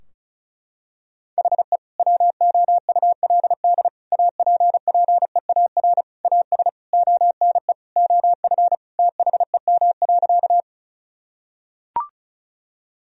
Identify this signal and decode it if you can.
Morse code, heard as a keyed tone, about 34 wpm, 700 Hz. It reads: HE WOULD APPEAR AS ONE OF THEM.
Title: Sentences from Top 1000 Words - Review Part 2 - 35wpm